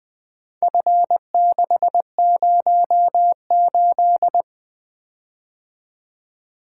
Morse code, heard as a keyed tone, about 22 wpm, 700 Hz. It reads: F608
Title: Speed-Racing - Four Letter-Number Combinations - 20wpm